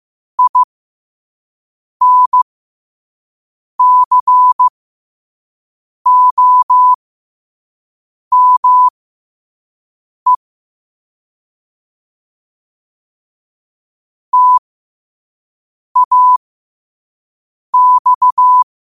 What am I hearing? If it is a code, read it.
Morse code, heard as a keyed tone, about 15 wpm, spaced as for 5 wpm, 1000 Hz. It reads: INCOME TAX